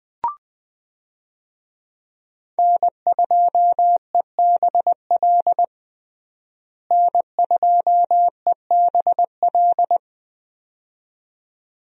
Morse code, heard as a keyed tone, about 21 wpm, 700 Hz. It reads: N2EBL N2EBL
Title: USA Call Signs - 5 Characters - Repeated 2x - 20wpm